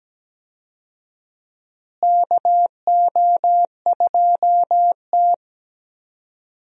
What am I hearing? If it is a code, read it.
Morse code, heard as a keyed tone, about 17 wpm, 700 Hz. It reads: KO2T
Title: USA Call Signs - 4 Characters - Repeated 3x - 17wpm